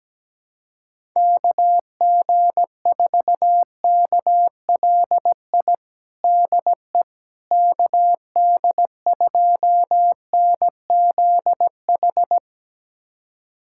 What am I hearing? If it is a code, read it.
Morse code, heard as a keyed tone, about 17 wpm, 700 Hz. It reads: KG4KLI DE KD2NZH
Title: USA Callsign DE Callsign 17wpm